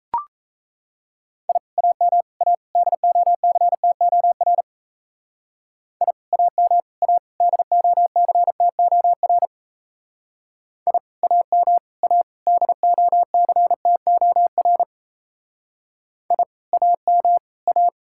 Morse code, heard as a keyed tone, about 32 wpm, 700 Hz. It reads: I AM A DOCTOR I AM A DOCTOR I AM A DOCTOR I AM A
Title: Speed-Racing Common QSO Phrases - 28wpm